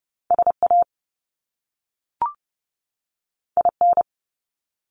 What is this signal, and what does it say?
Morse code, heard as a keyed tone, about 30 wpm, 700 Hz. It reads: SA IN
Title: Top Two Letter Combinations in Words 30wpm